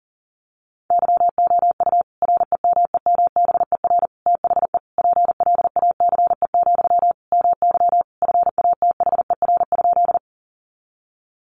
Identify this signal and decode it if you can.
Morse code, heard as a keyed tone, about 37 wpm, 700 Hz. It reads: YOU REMEMBER THE PLACE, MY FATHER?
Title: Sentences from Top 400 Words 40wpm